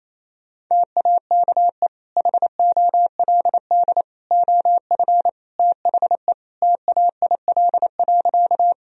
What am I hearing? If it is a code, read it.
Morse code, heard as a keyed tone, about 28 wpm, 700 Hz. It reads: TAKE HOLD OF THE TAIL.
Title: Sentences from Top 1000 Words - Review Part 3 - 28wpm